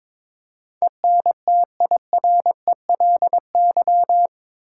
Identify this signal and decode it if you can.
Morse code, heard as a keyed tone, about 21 wpm, 700 Hz. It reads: ENTIRELY
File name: practice / 22wpm / Eight Letter Words - 22wpm